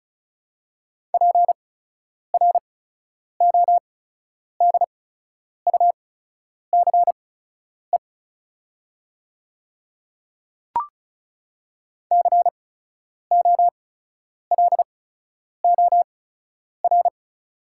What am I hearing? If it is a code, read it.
Morse code, heard as a keyed tone, about 34 wpm, spaced as for 10 wpm, 690 Hz. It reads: PRODUCE COLOR
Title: 8x Character Spacing - Top 500 Words - 35wpm